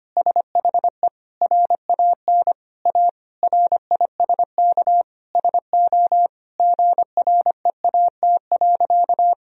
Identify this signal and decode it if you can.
Morse code, heard as a keyed tone, about 25 wpm, 700 Hz. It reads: SHE RAN A RISK SO GREAT.
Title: Sentences from Top 1000 Words - Review Part 4 - 25wpm